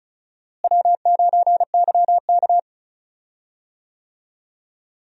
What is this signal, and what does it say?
Morse code, heard as a keyed tone, about 36 wpm, 700 Hz. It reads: W9YK
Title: USA Call Signs - 4 Characters - Repeated 3x - 35wpm